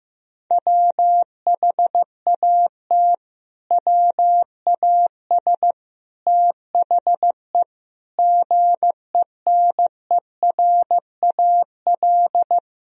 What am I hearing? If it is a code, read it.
Morse code, heard as a keyed tone, about 15 wpm, 700 Hz. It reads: WHAT WAS THE GENERAL
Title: Sentences from Top 1000 Words - Review Part 3 - 15wpm